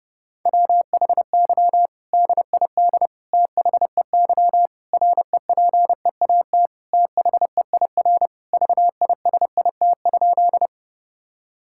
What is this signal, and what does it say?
Morse code, heard as a keyed tone, about 30 wpm, 700 Hz. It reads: WHY DID THEY REPEAT THEIR VISIT?
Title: Sentences from Top 1600 Words 30wpm